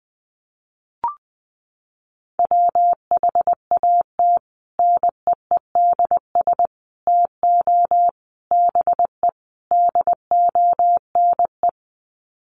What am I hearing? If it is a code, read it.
Morse code, heard as a keyed tone, about 20 wpm, 700 Hz. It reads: WHAT NEEDS TO BE DONE
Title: Sets of 5 Words Encore 20wpm